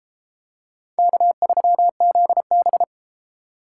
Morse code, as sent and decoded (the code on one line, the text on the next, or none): -.- ...-- --.. -...
K3ZB